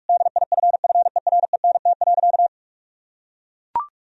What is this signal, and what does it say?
Morse code, heard as a keyed tone, about 42 wpm, 700 Hz. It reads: DIFFERENT.
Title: Sentences from Top 1000 Words - Review Part 3 - 45wpm